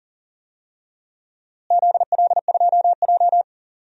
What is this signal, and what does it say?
Morse code, heard as a keyed tone, about 40 wpm, 700 Hz. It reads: ZL2J
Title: DXCC-100 Call Signs - 4 Characters - Repeated 3x - 40wpm